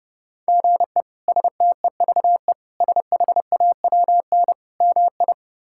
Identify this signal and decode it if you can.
Morse code, heard as a keyed tone, about 30 wpm, 700 Hz. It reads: GE STEVE SHAWN MI
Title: K1USN SST QSO Practice - 30wpm